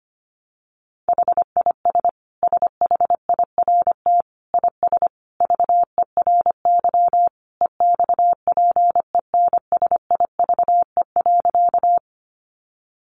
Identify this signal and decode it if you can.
Morse code, heard as a keyed tone, about 25 wpm, 700 Hz. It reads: HIS SHIRT IS VERY EXPENSIVE.
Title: Additional Easy Phrases 25wpm